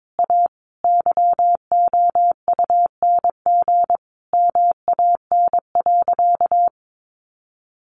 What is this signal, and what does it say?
Morse code, heard as a keyed tone, about 22 wpm, 700 Hz. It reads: A YOUNG MAN.
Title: Sentences from Top 500 Words 22wpm